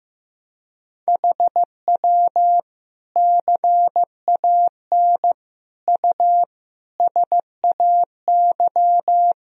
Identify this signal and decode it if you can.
Morse code, heard as a keyed tone, about 15 wpm, 700 Hz. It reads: HW CAN U SAY
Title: Sentences from Top 200 Words and Abbreviations 15wpm